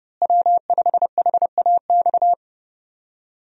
Morse code, heard as a keyed tone, about 30 wpm, 700 Hz. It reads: W5HAX
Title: USA Call Signs - 5 Characters - Repeated 3x - 30wpm